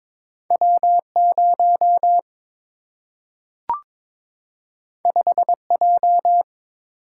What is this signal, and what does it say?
Morse code, heard as a keyed tone, about 22 wpm, 700 Hz. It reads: W 0 5 J
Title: Single-Letter Single-Number 22wpm